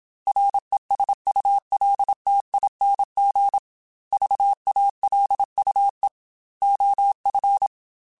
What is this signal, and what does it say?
Morse code, heard as a keyed tone, about 27 wpm, 800 Hz. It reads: RESULTING VALUE OF
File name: Technical Subjects / Technical Subjects-mp3s / XC @25